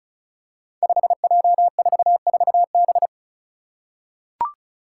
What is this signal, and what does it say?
Morse code, heard as a keyed tone, about 36 wpm, 700 Hz. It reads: 5J44B
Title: Five Challenging Letter-Number Combinations 35wpm